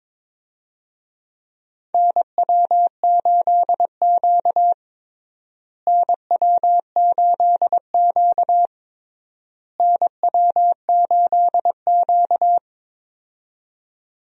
Morse code, heard as a keyed tone, about 22 wpm, 700 Hz. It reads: NW8Q NW8Q NW8Q
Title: USA Call Signs - 4 Characters - Repeated 3x - 22wpm